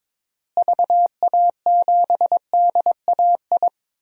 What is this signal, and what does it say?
Morse code, heard as a keyed tone, about 22 wpm, 700 Hz. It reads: VA7DAI
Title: DXCC-100 Call Signs - 6 Characters - Repeated 3x - 22wpm